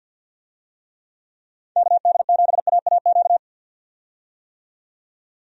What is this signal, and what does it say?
Morse code, heard as a keyed tone, about 49 wpm, 700 Hz. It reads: KD7AAQ
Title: USA Call Signs - 6 Characters - 50wpm